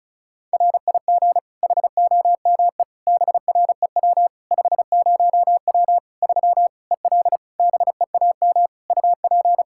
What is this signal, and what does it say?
Morse code, heard as a keyed tone, about 36 wpm, 700 Hz. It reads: RIG HOME BREW 50W 3 EL BEAM UP
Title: Practice QSO - Exchange 2 - Info - 35wpm